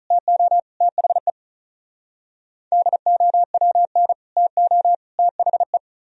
Morse code, heard as a keyed tone, about 46 wpm, 700 Hz. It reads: TO THE DOWN TO THE